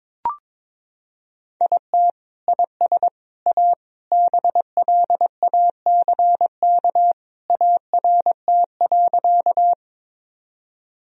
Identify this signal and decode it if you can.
Morse code, heard as a keyed tone, about 22 wpm, 700 Hz. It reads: IT IS A BLACK ART.
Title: Sentences from Top 1000 Words - Review Part 3 - 22wpm